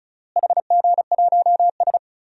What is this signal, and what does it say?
Morse code, heard as a keyed tone, about 36 wpm, 700 Hz. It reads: HG1S
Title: DXCC-100 Call Signs - 4 Characters - Repeated 2x - 35wpm